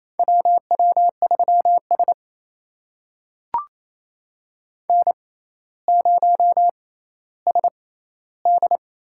Morse code, heard as a keyed tone, about 28 wpm, spaced as for 14 wpm, 700 Hz. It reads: WW3S N0SD
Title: USA Call Signs - 6x Character Spacing - 28wpm